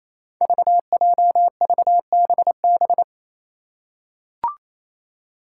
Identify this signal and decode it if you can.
Morse code, heard as a keyed tone, about 28 wpm, 700 Hz. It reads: VJVBB
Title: Five Challenging Letter-Number Combinations 28wpm